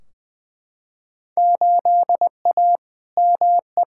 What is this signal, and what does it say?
Morse code, heard as a keyed tone, about 20 wpm, 700 Hz. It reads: 8A ME
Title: Field Day Exchanges - 20wpm